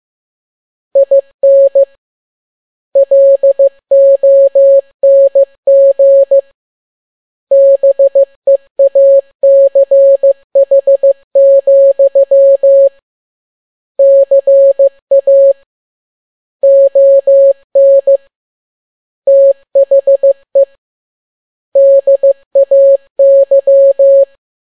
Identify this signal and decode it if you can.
Morse code, heard as a keyed tone, about 15 wpm, 550 Hz. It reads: IN LONG BEACH, CA ON THE DAY
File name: n6tt p1 15C13